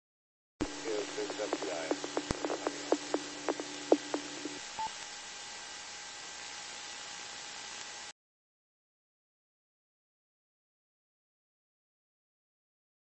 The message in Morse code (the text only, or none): none